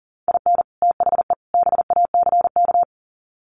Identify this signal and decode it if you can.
Morse code, heard as a keyed tone, about 40 wpm, 700 Hz. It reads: IN THE BACK
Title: Sets of 3 Words 40wpm